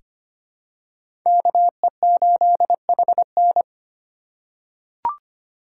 Morse code, heard as a keyed tone, about 25 wpm, 700 Hz. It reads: KE8HN
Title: USA Call Signs - 5 Characters - 4x Character Spacing - 25wpm